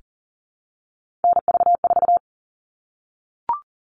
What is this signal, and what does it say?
Morse code, heard as a keyed tone, about 40 wpm, 700 Hz. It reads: NV4
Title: All Cut Numbers - 3-Digit 40wpm